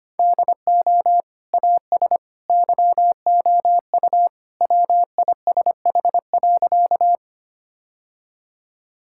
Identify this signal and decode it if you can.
Morse code, heard as a keyed tone, about 25 wpm, 700 Hz. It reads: DO AS YOU WISH.
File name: practice / 25wpm / Sentences with 2-Word Phrases - 25wpm